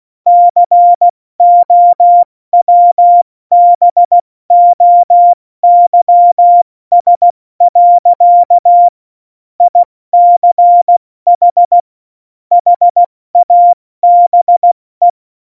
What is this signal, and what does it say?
Morse code, heard as a keyed tone, about 16 wpm, 700 Hz. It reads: COWBOYS. ICH HABE
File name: Morsekurs / Witze / Witz 01. Dezember 2020, 70 F80